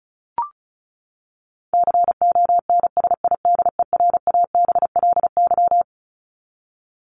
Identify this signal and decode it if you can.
Morse code, heard as a keyed tone, about 34 wpm, 700 Hz. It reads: CONSIDERABLY
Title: LY Suffix - Top 100 Words with Base Words 35wpm